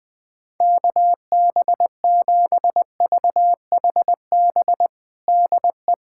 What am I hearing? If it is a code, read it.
Morse code, heard as a keyed tone, about 20 wpm, 700 Hz. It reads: KB7VHB DE
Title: USA Callsign DE Callsign - 6 Characters - 20wpm